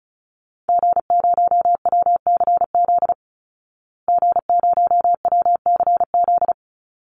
Morse code, heard as a keyed tone, about 34 wpm, 700 Hz. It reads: G0WCZ G0WCZ
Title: Call Signs Repeated 3x - 35wpm